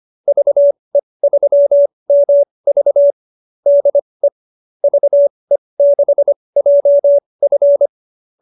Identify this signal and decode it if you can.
Morse code, heard as a keyed tone, about 25 wpm, spaced as for 20 wpm, 570 Hz. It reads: VE3MV DE VE6JF